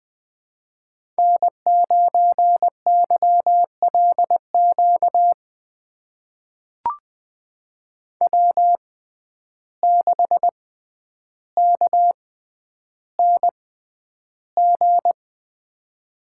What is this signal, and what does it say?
Morse code, heard as a keyed tone, about 20 wpm, spaced as for 10 wpm, 700 Hz. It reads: N9YLQ W6KNG
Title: USA Call Signs - 6x Character Spacing - 20wpm